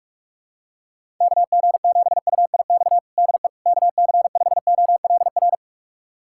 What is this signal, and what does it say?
Morse code, heard as a keyed tone, about 43 wpm, 700 Hz. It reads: KG7UIX DE KC5OLR
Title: USA Callsign DE Callsign - 6 Characters - 45wpm